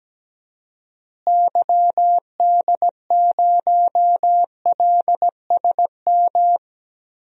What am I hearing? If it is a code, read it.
Morse code, heard as a keyed tone, about 17 wpm, 700 Hz. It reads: YD0LSM
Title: DXCC-100 Call Signs - 6 Characters - Repeated 3x - 17wpm